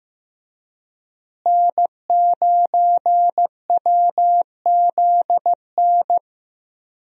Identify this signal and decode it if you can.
Morse code, heard as a keyed tone, about 15 wpm, 700 Hz. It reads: N9WZN
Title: USA Call Signs - 5 Characters - Repeated 3x - 15wpm